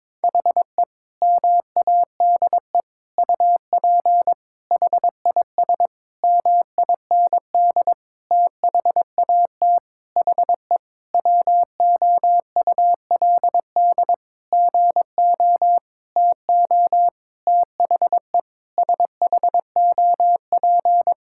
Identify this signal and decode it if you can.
Morse code, heard as a keyed tone, about 22 wpm, 700 Hz. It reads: HE MADE UP HIS MIND THAT HE WOULD GO TO THE SHOP
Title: Sentences from Top 1600 Words 22wpm